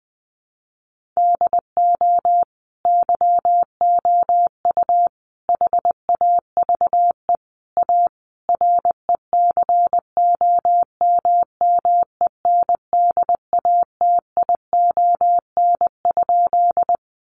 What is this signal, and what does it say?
Morse code, heard as a keyed tone, about 20 wpm, 700 Hz. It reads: DO YOU HAVE A RECOMMENDATION?
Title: Sentences from Top 1400 Words 20wpm